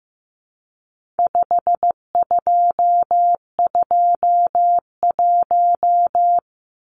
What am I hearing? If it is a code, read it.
Morse code, heard as a keyed tone, about 15 wpm, 700 Hz. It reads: 5221
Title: Numbers - 4-Digit 15wpm